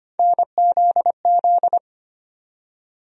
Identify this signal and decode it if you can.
Morse code, heard as a keyed tone, about 25 wpm, 700 Hz. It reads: NZZ
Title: USA Call Sign Postfixes - 3 Letters 25wpm